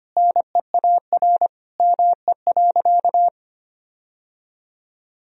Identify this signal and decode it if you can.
Morse code, heard as a keyed tone, about 25 wpm, 700 Hz. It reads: NEAR ME.
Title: Sentences from Top 1000 Words - Review Part 1 - 25wpm